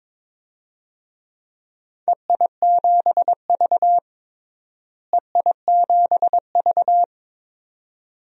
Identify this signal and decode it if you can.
Morse code, heard as a keyed tone, about 22 wpm, 700 Hz. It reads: EI7V EI7V
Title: DXCC-100 Call Signs - 4 Characters - Repeated 2x - 22wpm